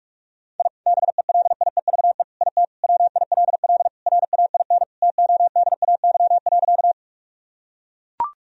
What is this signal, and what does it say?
Morse code, heard as a keyed tone, about 42 wpm, 700 Hz. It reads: I BELIEVE IT WILL RAIN TODAY.
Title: Sentences from Top 1000 Words - Review Part 1 - 45wpm